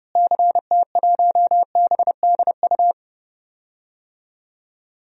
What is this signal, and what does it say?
Morse code, heard as a keyed tone, about 30 wpm, 700 Hz. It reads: CT1BDU
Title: DXCC-100 Call Signs - 6 Characters - Repeated 2x - 30wpm